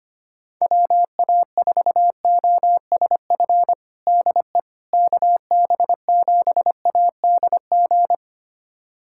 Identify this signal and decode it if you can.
Morse code, heard as a keyed tone, about 25 wpm, 700 Hz. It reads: WA4OSF DE KB7ADG
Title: USA Callsign DE Callsign - 6 Characters - 25wpm